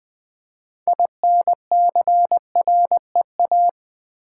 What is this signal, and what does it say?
Morse code, heard as a keyed tone, about 20 wpm, 700 Hz. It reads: INCREA
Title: Build A Word - 6 Letters At A Time - 12-Letter Words - 20wpm